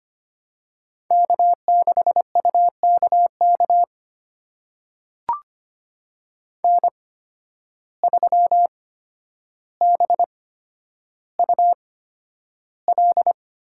Morse code, heard as a keyed tone, about 25 wpm, spaced as for 11 wpm, 700 Hz. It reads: K6UKK N3BUL
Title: USA Call Signs - 5 Characters - 8x Character Spacing - 25wpm